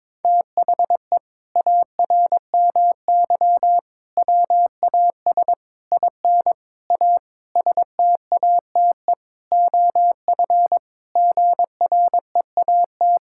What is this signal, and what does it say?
Morse code, heard as a keyed tone, about 22 wpm, 700 Hz. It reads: THE ARMY WAS IN A STATE OF GREAT
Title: Sentences from Top 1600 Words 22wpm